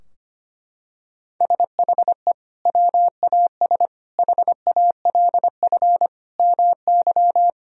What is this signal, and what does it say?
Morse code, heard as a keyed tone, about 25 wpm, 700 Hz. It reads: SHE WAS HALF MY
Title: Sentences from Top 1000 Words - Review Part 2 - 25wpm